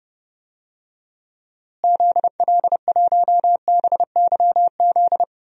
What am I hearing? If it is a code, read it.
Morse code, heard as a keyed tone, about 30 wpm, 700 Hz. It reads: ZL1BYZ
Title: DXCC-100 Call Signs - 6 Characters - Repeated 2x - 30wpm